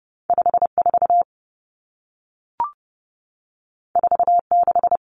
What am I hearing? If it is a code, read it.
Morse code, heard as a keyed tone, about 30 wpm, 700 Hz. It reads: 54 46
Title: Numbers - 2-Digit 30wpm